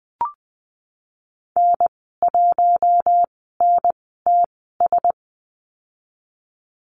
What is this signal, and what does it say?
Morse code, heard as a keyed tone, about 19 wpm, spaced as for 14 wpm, 700 Hz. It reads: N1NTS
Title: 062 Lesson 11g - Letter h - Call Signs 20wpm